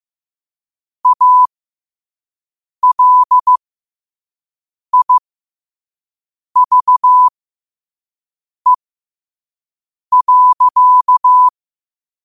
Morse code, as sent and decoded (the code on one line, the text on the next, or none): .- .-.. .. ...- . .-.-.-
ALIVE.